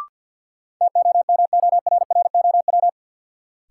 Morse code, heard as a keyed tone, about 52 wpm, 700 Hz. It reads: TOMORROW
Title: Sentences from Top 1100 Words - New Word Drill Alt - 50wpm